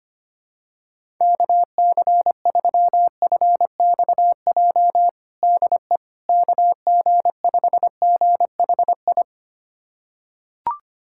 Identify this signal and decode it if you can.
Morse code, heard as a keyed tone, about 25 wpm, 700 Hz. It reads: KC3FXJ DE KG5GHI
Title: USA Callsign DE Callsign 25wpm